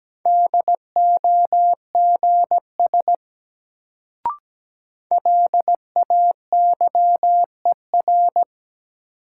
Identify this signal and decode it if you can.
Morse code, heard as a keyed tone, about 17 wpm, 700 Hz. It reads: DOGS LAYER
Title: Sentences from Top 1600 Words - New Word Drill - 17wpm